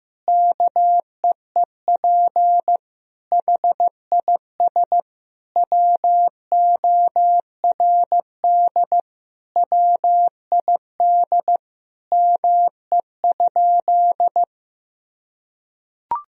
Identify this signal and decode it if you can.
Morse code, heard as a keyed tone, about 15 wpm, 700 Hz. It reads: KEEP HIS WORD WID ME?
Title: Sentences from Top 300 Words and Abbreviations 15wpm